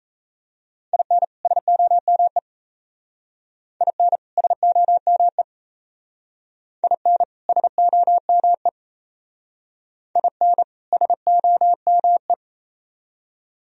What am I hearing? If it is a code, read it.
Morse code, heard as a keyed tone, about 35 wpm, 700 Hz. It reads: IN SOME IN SOME IN SOME IN SOME
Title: Speed-Racing - Sets of 2 Words 28wpm